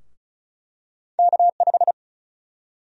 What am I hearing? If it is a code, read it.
Morse code, heard as a keyed tone, about 34 wpm, 700 Hz. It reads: K5
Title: USA Call Sign Prefixes - Repeated 2x - 35wpm